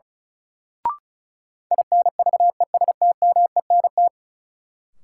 Morse code, heard as a keyed tone, about 34 wpm, 700 Hz. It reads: INVESTMENT
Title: Sentences from Top 700 Words - New Word Drill - 35wpm